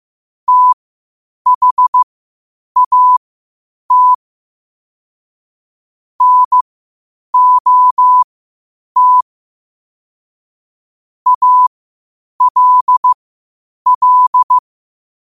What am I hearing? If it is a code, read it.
Morse code, heard as a keyed tone, about 15 wpm, spaced as for 8 wpm, 1000 Hz. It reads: THAT NOT ALL
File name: qotdfiles / 8wpm